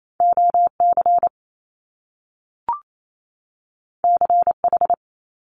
Morse code, heard as a keyed tone, about 28 wpm, 700 Hz. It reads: OC CH
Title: Top Two Letter Combinations in Words 28wpm